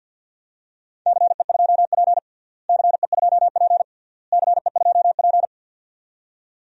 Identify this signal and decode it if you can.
Morse code, heard as a keyed tone, about 51 wpm, 700 Hz. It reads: CE2P CE2P CE2P
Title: DXCC-100 Call Signs - 4 Characters - Repeated 3x - 50wpm